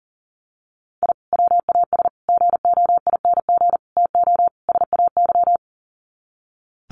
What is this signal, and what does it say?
Morse code, heard as a keyed tone, about 40 wpm, 700 Hz. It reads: I WAS GOING TO SAY
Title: Sets of 5 Words 40wpm